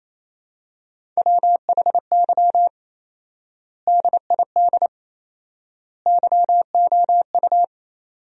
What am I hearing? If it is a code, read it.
Morse code, heard as a keyed tone, about 28 wpm, 700 Hz. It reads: WHY DID YOU
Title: 4x Word-Spacing - Sentences from Top 500 Words - 28wpm